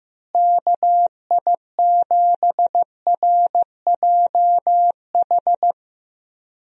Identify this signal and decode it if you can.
Morse code, heard as a keyed tone, about 15 wpm, 700 Hz. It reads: KI7RJH
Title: USA Call Signs - 6 Characters - Repeated 3x - 15wpm